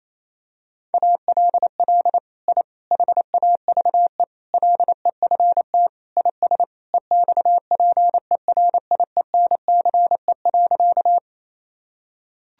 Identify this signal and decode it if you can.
Morse code, heard as a keyed tone, about 28 wpm, 700 Hz. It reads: ALL I HAVE LEFT IS EXPERIENCE.
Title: Sentences from Top 1000 Words - Review Part 3 - 28wpm